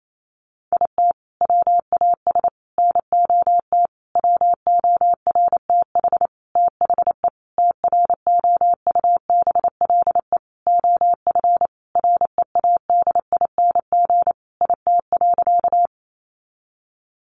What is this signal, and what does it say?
Morse code, heard as a keyed tone, about 28 wpm, 700 Hz. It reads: IT WAS NOT WORTH THE TROUBLE OF READING IT.
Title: Sentences from Top 1200 Words 28wpm